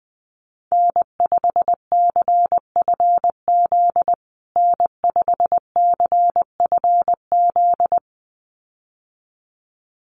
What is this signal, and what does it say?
Morse code, heard as a keyed tone, about 20 wpm, 700 Hz. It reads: N5CFZ N5CFZ
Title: USA Call Signs Repeated 2x 20wpm